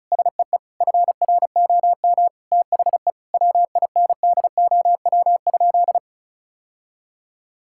Morse code, heard as a keyed tone, about 35 wpm, 700 Hz. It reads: SEE FROM THE WINDOW?
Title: Sentences from Top 1600 Words 35wpm